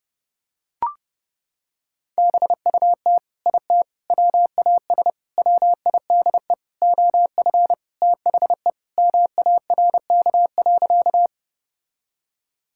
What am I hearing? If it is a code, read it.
Morse code, heard as a keyed tone, about 30 wpm, 700 Hz. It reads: BUT IT WAS WIDE OF THE MARK.
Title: Sentences from Top 1000 Words - Review Part 6 - 30wpm